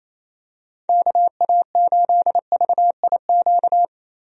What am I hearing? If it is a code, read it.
Morse code, heard as a keyed tone, about 28 wpm, 700 Hz. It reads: KA8VIQ
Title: USA Call Signs - 6 Characters - 28wpm